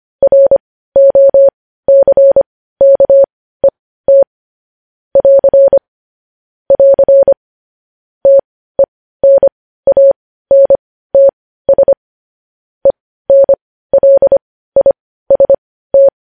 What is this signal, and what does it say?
Morse code, heard as a keyed tone, about 25 wpm, spaced as for 15 wpm, 550 Hz. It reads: ROCKET <AR> <AR> TENANTS ENLIST
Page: News Headlines in Morse Code at 15 WPM